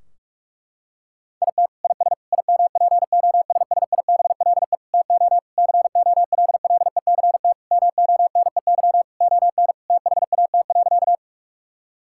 Morse code, heard as a keyed tone, about 46 wpm, 700 Hz. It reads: IT IS IMPOSSIBLE TO COLLECT MONEY ON THAT.
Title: Sentences from Top 1600 Words 45wpm